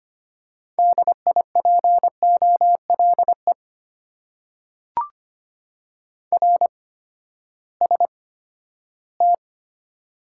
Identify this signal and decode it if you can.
Morse code, heard as a keyed tone, about 25 wpm, spaced as for 14 wpm, 700 Hz. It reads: DIPOLE RST
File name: Top 90 QSO Elements - 8x Character Spacing - 25wpm